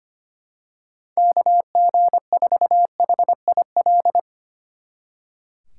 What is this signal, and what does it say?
Morse code, heard as a keyed tone, about 26 wpm, 700 Hz. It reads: KG4HIL
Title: Speed Racing - USA Call Signs 25wpm